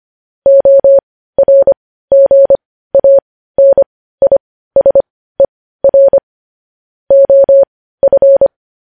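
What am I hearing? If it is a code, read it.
Morse code, heard as a keyed tone, about 25 wpm, spaced as for 15 wpm, 550 Hz. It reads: ORGANISER OF
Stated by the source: News headlines sent as Morse practice at 15.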